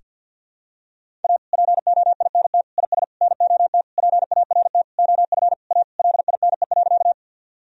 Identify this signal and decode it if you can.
Morse code, heard as a keyed tone, about 52 wpm, 700 Hz. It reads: A POINT IS NOT PART OF A LINE.
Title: Sentences from Top 1000 Words - Review Part 1 - 50wpm